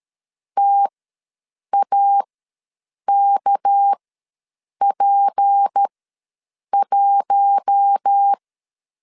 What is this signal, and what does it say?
Morse code, heard as a keyed tone, about 13 wpm, spaced as for 7 wpm, 800 Hz. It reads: TAKP1
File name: LEZIONI / 7 GF9Z / 7 GF9 ws 12wpm 3344